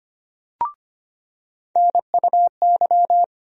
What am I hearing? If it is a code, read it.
Morse code, heard as a keyed tone, about 25 wpm, 700 Hz. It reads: NUY
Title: USA Call Sign Postfixes - 3 Letters 25wpm